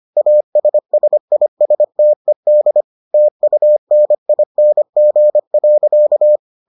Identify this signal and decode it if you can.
Morse code, heard as a keyed tone, about 25 wpm, 600 Hz. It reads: ASSISTED TUNING.